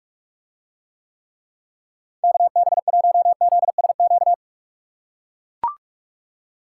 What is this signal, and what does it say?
Morse code, heard as a keyed tone, about 44 wpm, 700 Hz. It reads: KB1ZSQ